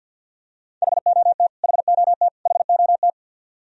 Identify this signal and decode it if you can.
Morse code, heard as a keyed tone, about 51 wpm, 700 Hz. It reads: HOT HOT HOT
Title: Top 100 Words Repeated 50wpm